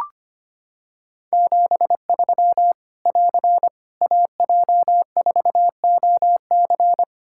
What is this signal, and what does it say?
Morse code, heard as a keyed tone, about 25 wpm, 700 Hz. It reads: 73 <AR> AJ4OC
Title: Practice QSO - Exchange 4 - Ending - 25wpm